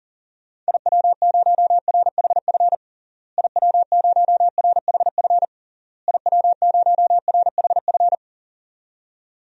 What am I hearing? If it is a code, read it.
Morse code, heard as a keyed tone, about 40 wpm, 700 Hz. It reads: IW0RHF IW0RHF IW0RHF
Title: DXCC-100 Call Signs - 6 Characters - Repeated 3x - 40wpm